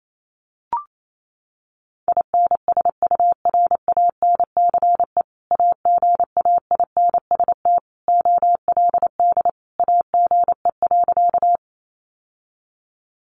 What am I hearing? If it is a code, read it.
Morse code, heard as a keyed tone, about 28 wpm, 700 Hz. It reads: INSURANCE AGAINST OLD AGE.
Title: Sentences from Top 1100 Words 28wpm